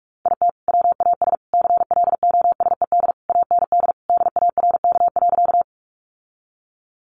Morse code, heard as a keyed tone, about 44 wpm, 700 Hz. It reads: IT WAS CLOSED AND DARK.
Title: Sentences from Top 1400 Words 45wpm